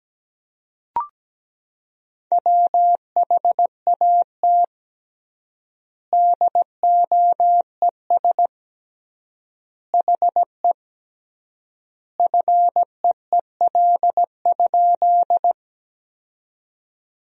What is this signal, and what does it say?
Morse code, heard as a keyed tone, about 17 wpm, 700 Hz. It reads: WHAT DOES HE FEEL?
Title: 3x Word-Spacing - Sentences from Top 400 Words - 17wpm